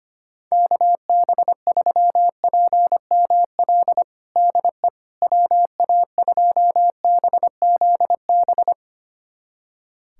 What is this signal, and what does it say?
Morse code, heard as a keyed tone, about 25 wpm, 700 Hz. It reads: KB3PML DE WA2BZB
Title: USA Callsign DE Callsign - 6 Characters - 25wpm